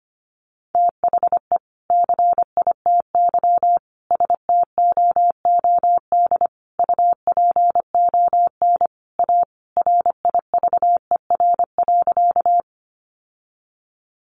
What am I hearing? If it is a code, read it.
Morse code, heard as a keyed tone, about 25 wpm, 700 Hz. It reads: THE CITY STOOD UPON A RIVER.
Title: Sentences from Top 1100 Words 25wpm